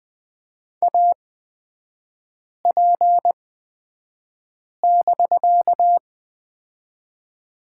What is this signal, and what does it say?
Morse code, heard as a keyed tone, about 20 wpm, spaced as for 5 wpm, 700 Hz. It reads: AP<BK>
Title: 250 Lesson 40e - BK - Three Characters 20wpm